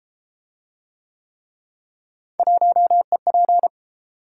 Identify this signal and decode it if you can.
Morse code, heard as a keyed tone, about 30 wpm, 700 Hz. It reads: 1EP